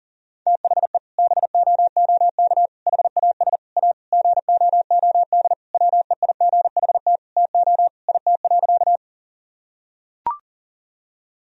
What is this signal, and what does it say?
Morse code, heard as a keyed tone, about 37 wpm, 700 Hz. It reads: THE BOOK HAS A GOOD WEIGHT TO IT.
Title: Sentences from Top 1000 Words - Review Part 3 - 40wpm